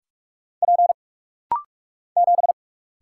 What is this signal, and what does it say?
Morse code, heard as a keyed tone, about 44 wpm, 700 Hz. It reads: P 7
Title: Single Letter-Number - Rapid-Fire 45wpm